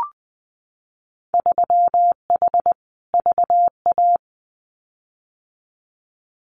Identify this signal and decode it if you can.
Morse code, heard as a keyed tone, about 20 wpm, 700 Hz. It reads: 3H VA
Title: Winter Field Day Exchanges Encore 20wpm